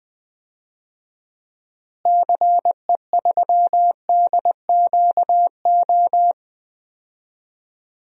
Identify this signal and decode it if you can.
Morse code, heard as a keyed tone, about 20 wpm, 700 Hz. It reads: CE3DQO
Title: DXCC-100 Call Signs - 6 Characters - Repeated 2x - 20wpm